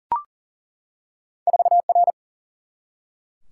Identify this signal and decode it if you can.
Morse code, heard as a keyed tone, about 40 wpm, 700 Hz. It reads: 4R
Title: Single-Letter Single-Number 40wpm